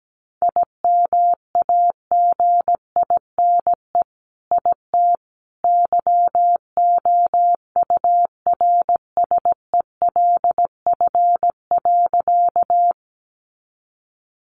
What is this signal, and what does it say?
Morse code, heard as a keyed tone, about 17 wpm, 700 Hz. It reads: IMAGINE IT YOURSELF.
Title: Sentences from Top 1400 Words 17wpm